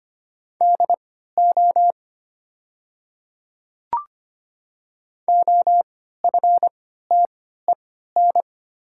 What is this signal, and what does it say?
Morse code, heard as a keyed tone, about 25 wpm, spaced as for 14 wpm, 700 Hz. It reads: DO OFTEN
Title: 3x Character Spacing - Top 300 Words - 25wpm